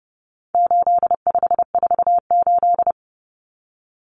Morse code, heard as a keyed tone, about 30 wpm, 700 Hz. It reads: 8548
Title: Numbers - 4-Digit 30wpm